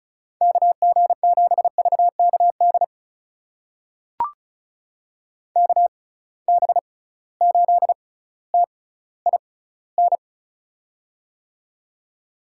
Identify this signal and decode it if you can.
Morse code, heard as a keyed tone, about 34 wpm, spaced as for 18 wpm, 700 Hz. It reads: KG7VKD KB8TIN
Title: USA Call Signs - 6 Characters - 6x Character Spacing - 35wpm